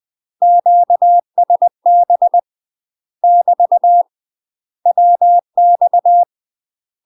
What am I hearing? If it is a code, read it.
Morse code, heard as a keyed tone, about 20 wpm, 700 Hz. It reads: QSB <BT> WX
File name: UK MASTER QSOs TRAINING / 20WPM / UK MASTER QSOs 0044